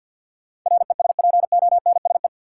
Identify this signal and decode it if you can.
Morse code, heard as a keyed tone, about 53 wpm, 700 Hz. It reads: RESPONSE
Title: Sentences from Top 1400 Words - New Word Drill Alt - 50wpm